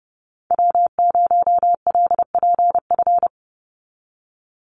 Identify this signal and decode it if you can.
Morse code, heard as a keyed tone, about 29 wpm, 700 Hz. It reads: W0LPF
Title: Call Signs Repeated 3x - 30wpm